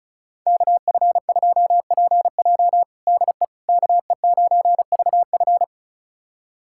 Morse code, heard as a keyed tone, about 36 wpm, 700 Hz. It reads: KF2PJ DE KE9VF
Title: USA Callsign DE Callsign - 5 Characters - 35wpm